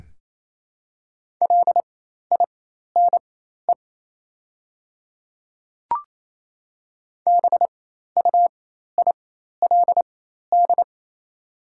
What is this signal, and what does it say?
Morse code, heard as a keyed tone, about 28 wpm, spaced as for 13 wpm, 700 Hz. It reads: LINE BUILD
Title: 4x Character Spacing - Top 200 Words - 28wpm